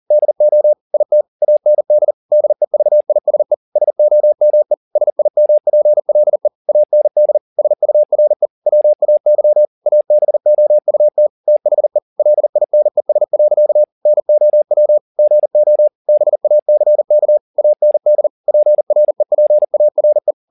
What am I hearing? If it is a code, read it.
Morse code, heard as a keyed tone, about 40 wpm, 600 Hz. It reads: DO IT AND DEVISE SOME SIMPLE AND SURE WAY ABOUT THE LINES. NOW GO BACK AND PREPARE
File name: the old man and the sea 2 40wpm